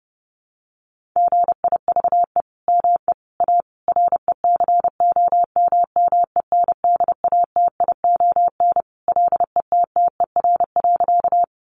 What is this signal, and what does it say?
Morse code, heard as a keyed tone, about 30 wpm, 700 Hz. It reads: GIVE ME A RECOMMENDATION LETTER.
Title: Sentences from Top 1400 Words 30wpm